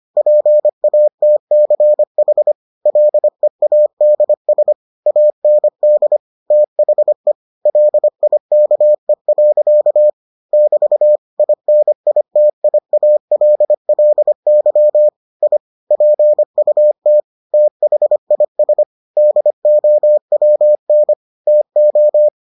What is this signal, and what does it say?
Morse code, heard as a keyed tone, about 25 wpm, 600 Hz. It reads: PATCH LEADS AND THE LIKE. <BT> INITIALLY I PUT THIS DOWN TO